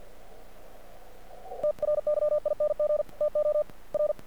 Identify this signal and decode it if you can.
Morse code, heard as a keyed tone, about 46 wpm, 600 Hz. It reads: TRYING TO R
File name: archived / CRSnet